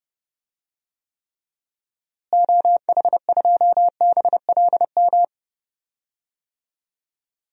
Morse code, heard as a keyed tone, about 31 wpm, 700 Hz. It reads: OH2BLM
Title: DXCC-100 Call Signs - 6 Characters - Repeated 2x - 30wpm